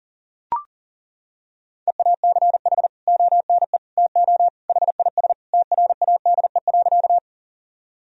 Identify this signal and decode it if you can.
Morse code, heard as a keyed tone, about 37 wpm, 700 Hz. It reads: EACH ONE TO HIS TRADE.
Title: Sentences from Top 1000 Words - Review Part 4 - 40wpm